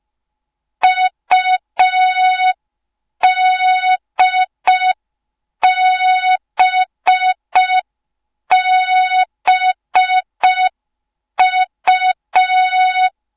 Morse code, heard as a keyed tone, about 5 wpm, 3700 Hz. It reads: UDBBU